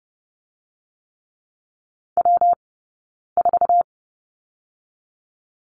Morse code, heard as a keyed tone, about 32 wpm, 700 Hz. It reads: W 4
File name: Lesson 19d - Letter w - Two Characters 30wpm